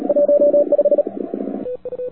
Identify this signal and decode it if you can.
Morse code, heard as a keyed tone, about 37 wpm, 580 Hz, with noise. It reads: 15
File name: wwcw08 / k5zd wwcw08 1300z